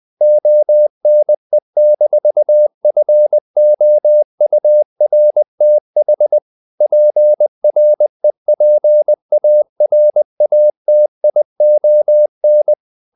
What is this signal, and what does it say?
Morse code, heard as a keyed tone, about 20 wpm, 600 Hz. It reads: ONE-FOURTH PREPARATION
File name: CW / Practice / EnglishQuotations0320